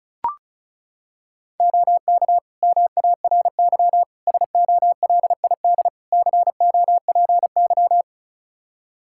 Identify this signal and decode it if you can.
Morse code, heard as a keyed tone, about 34 wpm, 700 Hz. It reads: OK MARY SOLID COPY
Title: Practice QSO - Exchange 3 - More Info - 35wpm